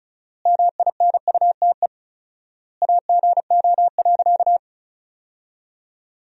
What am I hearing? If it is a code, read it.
Morse code, heard as a keyed tone, about 36 wpm, 700 Hz. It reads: MINUTE AGO.
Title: 4x Word-Spacing - Sentences from Top 500 Words - 35wpm